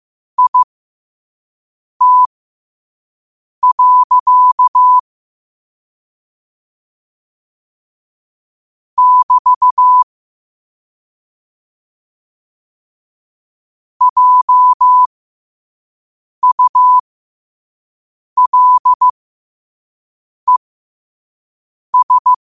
Morse code, heard as a keyed tone, about 15 wpm, spaced as for 5 wpm, 1000 Hz. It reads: IT. <BT> JULES
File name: qotdfiles / 5wpm